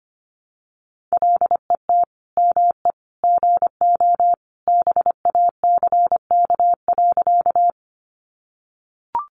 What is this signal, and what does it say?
Morse code, heard as a keyed tone, about 25 wpm, 700 Hz. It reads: LET ME GO BACK.
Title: Sentences from Top 300 Words 25wpm